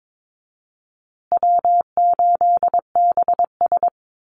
Morse code, heard as a keyed tone, about 22 wpm, 700 Hz. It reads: W8BS
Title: USA Call Signs Repeated 2x 22wpm